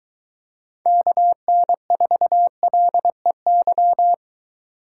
Morse code, heard as a keyed tone, about 23 wpm, 700 Hz. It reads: KN4LEY